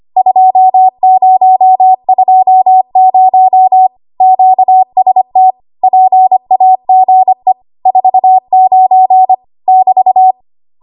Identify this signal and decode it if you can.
Morse code, heard as a keyed tone, about 25 wpm, 750 Hz. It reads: 2020 QST PAGE 49 <BT>
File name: Archive / 25 WPM / 25WPM